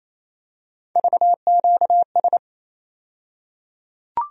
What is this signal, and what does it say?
Morse code, heard as a keyed tone, about 28 wpm, 700 Hz. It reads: VQS
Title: USA Call Sign Postfixes - 3 Letters 28wpm